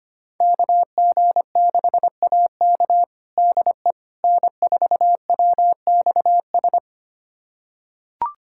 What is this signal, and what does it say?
Morse code, heard as a keyed tone, about 25 wpm, 700 Hz. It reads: KG6AK DE N4WXS
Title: USA Callsign DE Callsign - 5 Characters - 25wpm